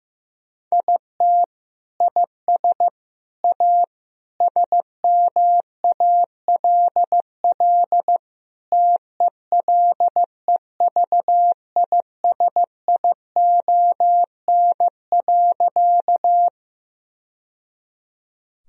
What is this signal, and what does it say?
Morse code, heard as a keyed tone, about 15 wpm, 700 Hz. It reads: IT IS A SMALL TELEVISION.
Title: Sentences from Top 1000 Words - Review Part 4 - 15wpm